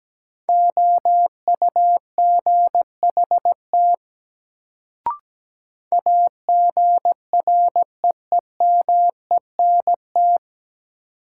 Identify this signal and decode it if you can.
Morse code, heard as a keyed tone, about 17 wpm, 700 Hz. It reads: OUGHT AGREEMENT
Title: Sets of 5 Words Encore - New Word Drill - 17wpm